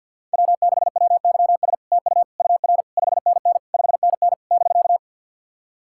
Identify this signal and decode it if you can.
Morse code, heard as a keyed tone, about 51 wpm, 700 Hz. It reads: W6WYS TU UR 5NN 5NN <BK>
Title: SOTA QSO Practice 50wpm